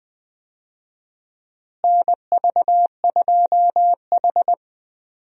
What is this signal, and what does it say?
Morse code, heard as a keyed tone, about 20 wpm, 700 Hz. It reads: NV2H
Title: USA Call Signs - 4 Characters - Repeated 2x - 20wpm